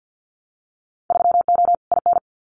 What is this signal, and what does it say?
Morse code, heard as a keyed tone, about 50 wpm, 700 Hz. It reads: 3O IN
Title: Winter Field Day Exchanges Encore 50wpm